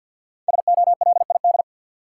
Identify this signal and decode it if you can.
Morse code, heard as a keyed tone, about 53 wpm, 700 Hz. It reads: SOLID
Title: Sentences from Top 1500 Words - New Word Drill - 50wpm